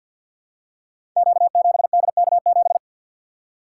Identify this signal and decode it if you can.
Morse code, heard as a keyed tone, about 52 wpm, 700 Hz. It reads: Q7DK7
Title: Code Groups - 5 Letter:Numbers - 50wpm